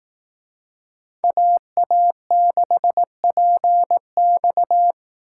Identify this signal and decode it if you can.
Morse code, heard as a keyed tone, about 18 wpm, 700 Hz. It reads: AA6PX
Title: Speed Racing - USA Call Signs 15wpm